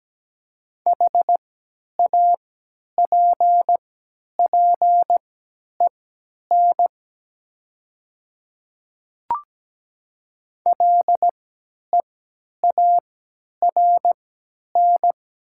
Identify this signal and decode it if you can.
Morse code, heard as a keyed tone, about 17 wpm, spaced as for 10 wpm, 690 Hz. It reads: HAPPEN LEARN
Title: 3x Character Spacing - Top 500 Words - 17wpm